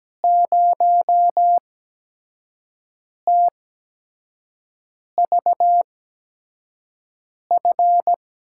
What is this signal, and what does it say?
Morse code, heard as a keyed tone, about 17 wpm, spaced as for 5 wpm, 700 Hz. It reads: 0TVF
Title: USA Call Signs - 6 Characters - 8x Character Spacing - 17wpm